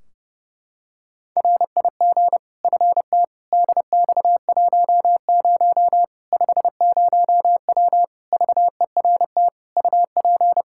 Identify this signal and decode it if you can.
Morse code, heard as a keyed tone, about 30 wpm, 700 Hz. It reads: RIG FT DX10 50W VERT UP
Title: Practice QSO - Exchange 2 - Info - 30wpm